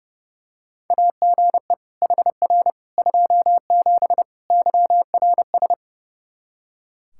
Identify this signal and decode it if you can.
Morse code, heard as a keyed tone, about 32 wpm, 700 Hz. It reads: AGE HR 27 YRS
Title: Practice QSO - Exchange 3 - More Info - 30wpm